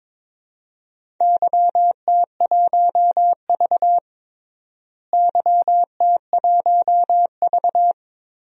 Morse code, heard as a keyed tone, about 22 wpm, 700 Hz. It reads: YT1V YT1V
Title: DXCC-100 Call Signs - 4 Characters - Repeated 2x - 22wpm